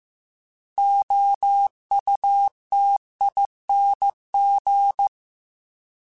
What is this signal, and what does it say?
Morse code, heard as a keyed tone, about 15 wpm, 800 Hz. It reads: OUTING